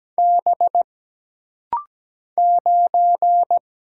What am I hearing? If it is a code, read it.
Morse code, heard as a keyed tone, about 17 wpm, 700 Hz. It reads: B 9
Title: Single Letter-Number - Rapid-Fire 17wpm